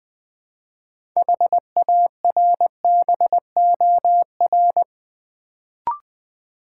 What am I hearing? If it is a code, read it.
Morse code, heard as a keyed tone, about 21 wpm, 700 Hz. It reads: HARBOR
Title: Build A Word - 3 Letters At A Time - 6-Letter Words - 20wpm